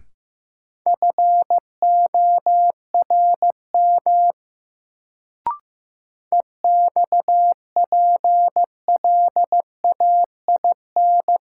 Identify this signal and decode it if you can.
Morse code, heard as a keyed tone, about 15 wpm, 700 Hz. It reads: FORM EXPLAIN
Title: Sentences from Top 1100 Words - New Word Drill Alt - 15wpm